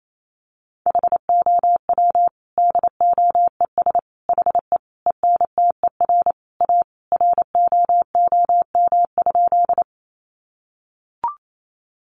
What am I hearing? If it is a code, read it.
Morse code, heard as a keyed tone, about 28 wpm, 700 Hz. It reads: HOW DOES HE ENTER A ROOM?
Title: Sentences from Top 1300 Words 28wpm